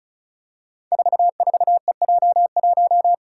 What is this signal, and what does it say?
Morse code, heard as a keyed tone, about 36 wpm, 700 Hz. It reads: 44EJ1
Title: Code Groups - 5 Letter:Numbers - 35wpm